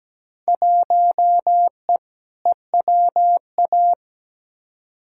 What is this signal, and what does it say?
Morse code, heard as a keyed tone, about 17 wpm, 700 Hz. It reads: 1E EWA
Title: Field Day Exchanges - 17wpm